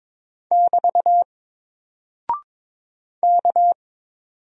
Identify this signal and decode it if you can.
Morse code, heard as a keyed tone, about 22 wpm, 700 Hz. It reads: <BT> K
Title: Single Character (Letter:Number:Punctuation:Prosign) Alt 22wpm